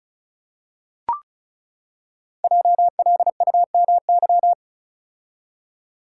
Morse code, heard as a keyed tone, about 36 wpm, 700 Hz. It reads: JLUMY
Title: Code Groups - 5 Letters - 35wpm